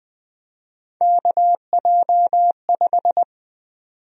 Morse code, heard as a keyed tone, about 20 wpm, 700 Hz. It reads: KJ5
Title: USA Call Sign Prefixes - 20wpm